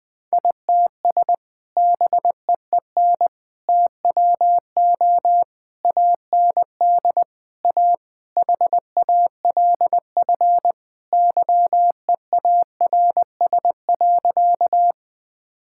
Morse code, heard as a keyed tone, about 21 wpm, 700 Hz. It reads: ITS BEEN TWO AND A HALF YEARS.
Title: Sentences with 5-Word Phrases - 20wpm